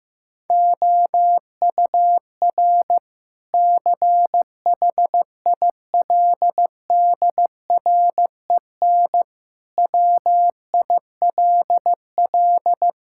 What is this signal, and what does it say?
Morse code, heard as a keyed tone, about 15 wpm, 700 Hz. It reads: OUR CHILDREN WILL
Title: Sentences from Top 1000 Words - Review Part 5 - 15wpm